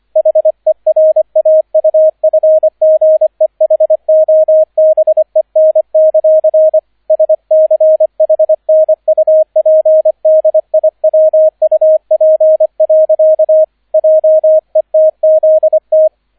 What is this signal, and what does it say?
Morse code, heard as a keyed tone, about 25 wpm, 630 Hz, with radio noise. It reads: HERAUFGEHOBEN; SCHNUPDIWUP. JETZT